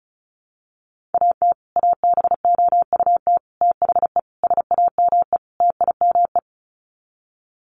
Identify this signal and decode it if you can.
Morse code, heard as a keyed tone, about 34 wpm, 700 Hz. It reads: AT ABOUT THE SAME TIME
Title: Sets of 5 Words Encore 35wpm